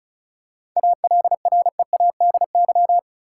Morse code, heard as a keyed tone, about 34 wpm, 700 Hz. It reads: ALREADY
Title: Seven Letter Words - 35wpm